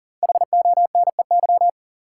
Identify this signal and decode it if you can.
Morse code, heard as a keyed tone, about 40 wpm, 700 Hz. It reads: HONEY
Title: Five Letter Words - 40wpm